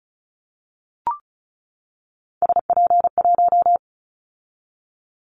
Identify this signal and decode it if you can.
Morse code, heard as a keyed tone, about 36 wpm, 700 Hz. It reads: SP1
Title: Callsign Prefixes for Top 25 DXCC - 35wpm